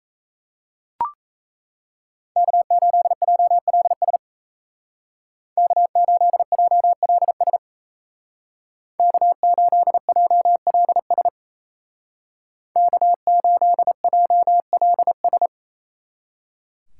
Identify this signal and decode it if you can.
Morse code, heard as a keyed tone, about 33 wpm, 700 Hz. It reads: K8JLS K8JLS K8JLS K8JLS
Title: Speed Racing - USA Call Signs 28wpm